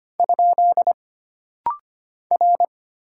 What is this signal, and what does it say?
Morse code, heard as a keyed tone, about 25 wpm, 700 Hz. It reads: ? R
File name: Single Character (Letter:Number:Punctuation:Prosign) - Rapid-Fire - 25wpm